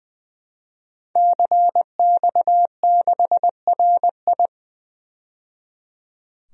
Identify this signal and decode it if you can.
Morse code, heard as a keyed tone, about 20 wpm, 700 Hz. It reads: CX6RI
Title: DXCC-100 Call Signs - 5 Characters - 20wpm